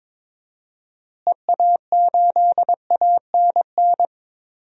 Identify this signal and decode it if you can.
Morse code, heard as a keyed tone, about 22 wpm, 700 Hz. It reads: EA8ANN
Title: DXCC-100 Call Signs - 6 Characters - Repeated 2x - 22wpm